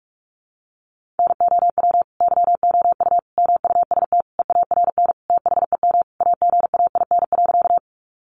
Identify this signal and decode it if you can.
Morse code, heard as a keyed tone, about 44 wpm, 700 Hz. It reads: NOW YOU MUST EARN THEM AGAIN.
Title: Sentences from Top 1500 Words 45wpm